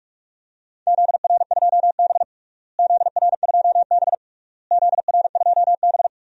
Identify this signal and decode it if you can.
Morse code, heard as a keyed tone, about 42 wpm, 690 Hz. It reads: ZR2B ZR2B ZR2B
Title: DXCC-100 Call Signs - 4 Characters - Repeated 3x - 45wpm